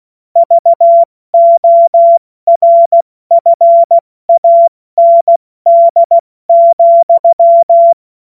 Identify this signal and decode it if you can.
Morse code, heard as a keyed tone, about 16 wpm, 680 Hz. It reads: VORFAND,
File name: Morsekurs / Witze / Witz 21.November 2023 70 F80